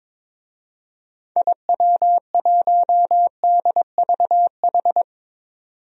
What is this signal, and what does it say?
Morse code, heard as a keyed tone, about 22 wpm, 700 Hz. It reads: IW1DVH
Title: DXCC-100 Call Signs - 6 Characters - 22wpm